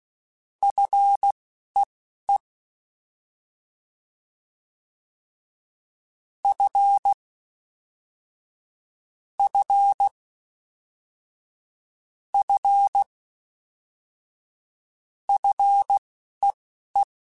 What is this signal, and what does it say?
Morse code, heard as a keyed tone, about 16 wpm, spaced as for 11 wpm, 800 Hz. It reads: FEE F F F FEE